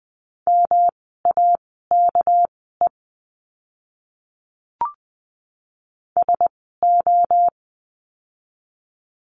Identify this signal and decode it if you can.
Morse code, heard as a keyed tone, about 20 wpm, spaced as for 15 wpm, 700 Hz. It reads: MAKE SO
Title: 230 Lesson 37f - Letter x - Words 20wpm